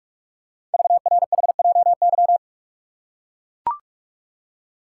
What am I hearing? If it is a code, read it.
Morse code, heard as a keyed tone, about 44 wpm, 690 Hz. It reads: VRHJY
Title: Code Groups - 5 Letter:Numbers - 45wpm